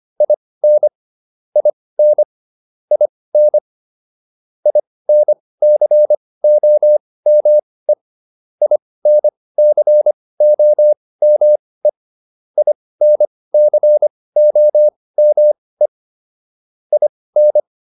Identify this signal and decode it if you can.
Morse code, heard as a keyed tone, about 25 wpm, spaced as for 18 wpm, 600 Hz. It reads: IN IN IN INCOME INCOME INCOME IN